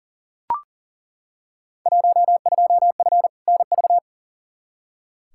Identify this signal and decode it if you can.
Morse code, heard as a keyed tone, about 40 wpm, 700 Hz. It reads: 12F NV
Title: Field Day Exchanges Encore - 40wpm